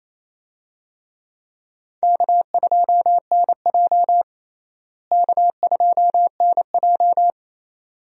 Morse code, heard as a keyed tone, about 28 wpm, 700 Hz. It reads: K2NJ K2NJ
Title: USA Call Signs - 4 Characters - Repeated 2x - 28wpm